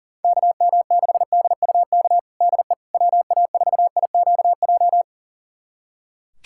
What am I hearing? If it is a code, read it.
Morse code, heard as a keyed tone, about 40 wpm, 700 Hz. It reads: KM6DUK DE WA4IQJ
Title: USA Callsign DE Callsign - 6 Characters - 40wpm